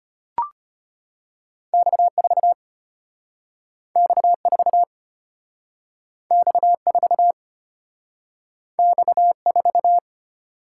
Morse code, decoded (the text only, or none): X4 X4 X4 X4